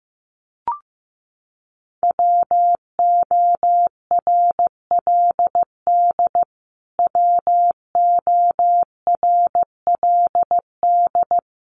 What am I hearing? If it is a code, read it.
Morse code, heard as a keyed tone, about 15 wpm, 700 Hz. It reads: WORLD WORLD
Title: Top 500 Words Repeated 15wpm